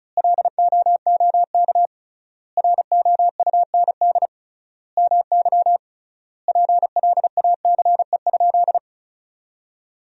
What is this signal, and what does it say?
Morse code, heard as a keyed tone, about 34 wpm, 700 Hz. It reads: LOOK ROUND MY PLACE?
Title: 3x Word-Spacing - Sentences from Top 200 Words - 35wpm